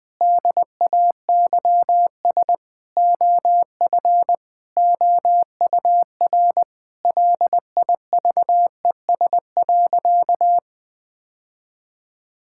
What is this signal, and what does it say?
Morse code, heard as a keyed tone, about 21 wpm, 700 Hz. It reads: DAYS OF OUR LIVES.
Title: Sentences with 2-Word Phrases - 20wpm